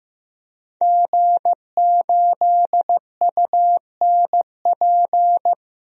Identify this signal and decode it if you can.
Morse code, heard as a keyed tone, about 15 wpm, 700 Hz. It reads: G8UNP
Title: DXCC-100 Call Signs - 5 Characters - 15wpm